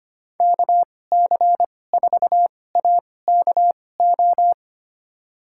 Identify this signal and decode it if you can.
Morse code, heard as a keyed tone, about 25 wpm, spaced as for 18 wpm, 700 Hz. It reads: KC4AKO
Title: USA Call Signs - 6 Characters - 2x Character Spacing - 25wpm